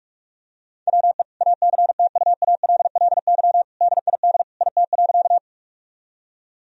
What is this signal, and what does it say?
Morse code, heard as a keyed tone, about 41 wpm, 700 Hz. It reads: WE ACTUALLY DID IT.
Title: Sentences from Top 1000 Words - Review Part 4 - 45wpm